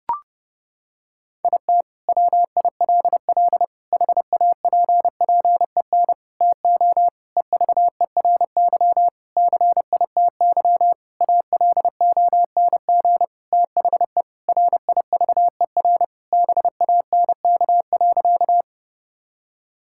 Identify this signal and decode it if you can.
Morse code, heard as a keyed tone, about 30 wpm, 700 Hz. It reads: IT WILL HAPPEN TO EVERY CITY ALONG THE RIVER BANK.
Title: Sentences from Top 1000 Words - Review Part 6 - 30wpm